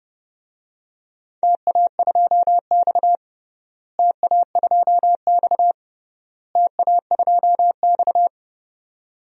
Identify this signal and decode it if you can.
Morse code, heard as a keyed tone, about 30 wpm, 700 Hz. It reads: TA2X TA2X TA2X
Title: DXCC-100 Call Signs - 4 Characters - Repeated 3x - 30wpm